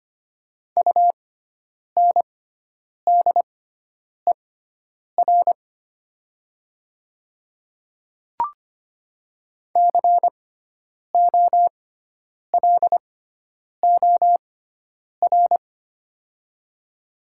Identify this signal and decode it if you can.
Morse code, heard as a keyed tone, about 25 wpm, spaced as for 9 wpm, 700 Hz. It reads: UNDER COLOR
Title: 6x Character Spacing - Top 300 Words - 25wpm